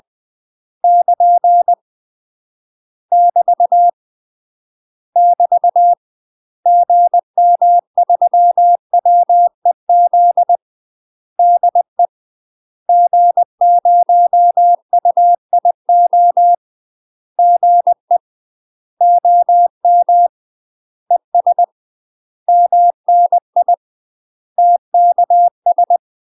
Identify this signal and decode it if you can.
Morse code, heard as a keyed tone, about 20 wpm, 700 Hz. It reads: <KN> <BT> <BT> GM3WEZ DE G0UIO GE OM ES MNI TKS
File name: UK MASTER QSOs TRAINING / 20WPM / UK MASTER QSOs 0007